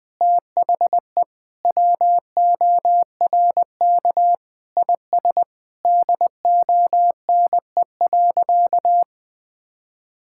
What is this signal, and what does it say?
Morse code, heard as a keyed tone, about 20 wpm, 700 Hz. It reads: THE WORK IS DONE.
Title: Sentences with 3-Word Phrases - 20wpm